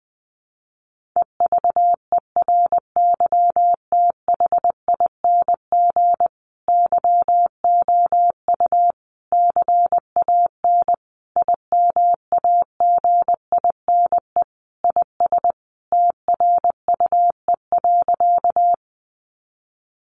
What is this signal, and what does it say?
Morse code, heard as a keyed tone, about 20 wpm, 700 Hz. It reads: EVERYTHING YOU CAN IMAGINE IS TRUE.
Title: Sentences from Top 1400 Words 20wpm